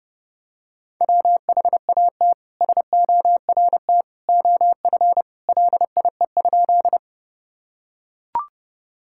Morse code, heard as a keyed tone, about 30 wpm, 700 Hz. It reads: WHAT SORT OF LIE?
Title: Sentences from Top 1600 Words 30wpm